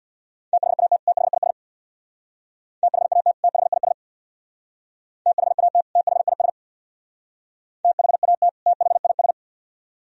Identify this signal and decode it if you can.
Morse code, heard as a keyed tone, about 66 wpm, 700 Hz. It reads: THAT THIS THAT THIS THAT THIS THAT THIS